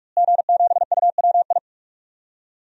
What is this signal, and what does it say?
Morse code, heard as a keyed tone, about 46 wpm, 700 Hz. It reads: G7UWI
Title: DXCC-100 Call Signs - 5 Characters - Repeated 3x - 45wpm